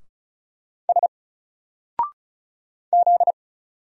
Morse code, heard as a keyed tone, about 34 wpm, 700 Hz. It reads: S Z
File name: Single-Letter-Number-35wpm